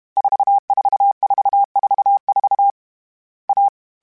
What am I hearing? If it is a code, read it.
Morse code, heard as a keyed tone, about 32 wpm, 800 Hz. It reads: 44444 A
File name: CW Code Groups-mp3s / Sets mp3s / AUV AUV4 sp @30